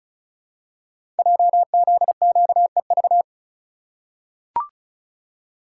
Morse code, heard as a keyed tone, about 36 wpm, 690 Hz. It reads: JZQEV
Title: Code Groups - 5 Letters - 35wpm